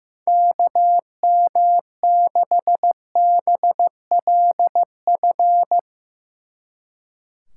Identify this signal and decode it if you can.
Morse code, heard as a keyed tone, about 15 wpm, 700 Hz. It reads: KM6BLF
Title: Speed Racing - USA 2x3 Call Signs 15wpm